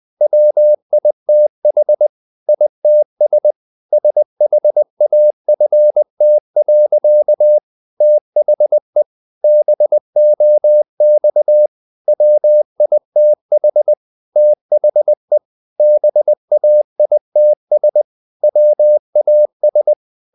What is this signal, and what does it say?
Morse code, heard as a keyed tone, about 20 wpm, 600 Hz. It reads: WITH ITS SHAFT. THE BOX WITH THE BAITS WAS
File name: the old man and the sea / the old man and the sea part 1 20wpm